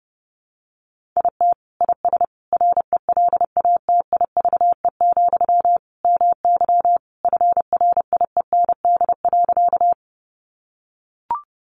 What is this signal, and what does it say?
Morse code, heard as a keyed tone, about 30 wpm, 700 Hz. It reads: IT IS RELATIVE, MY FRIEND.
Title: Sentences from Top 1400 Words 30wpm